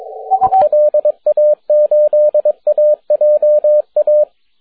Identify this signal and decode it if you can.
Morse code, heard as a keyed tone, about 22 wpm, 590 Hz, with radio noise. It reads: LA8AJA
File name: wp / 4l5o 6m cw june 5th 2010